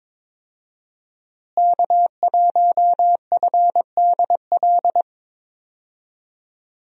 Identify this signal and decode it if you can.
Morse code, heard as a keyed tone, about 21 wpm, 700 Hz. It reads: K1FDL
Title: USA Call Signs - 5 Characters - Repeated 2x - 22wpm